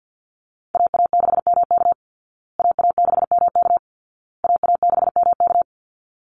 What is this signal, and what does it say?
Morse code, heard as a keyed tone, about 48 wpm, 700 Hz. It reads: AA6MK AA6MK AA6MK
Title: Call Signs Repeated 2x - 50wpm